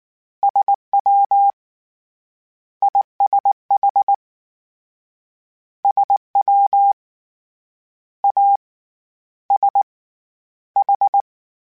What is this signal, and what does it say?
Morse code, heard as a keyed tone, about 19 wpm, spaced as for 10 wpm, 800 Hz. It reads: SWISH SWASH